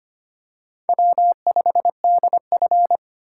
Morse code, heard as a keyed tone, about 25 wpm, 700 Hz. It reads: W5DF
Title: USA Call Signs - 4 Characters - 3x Character Spacing - 25wpm